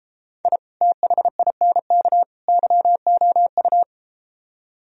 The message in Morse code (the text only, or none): I THINK YOU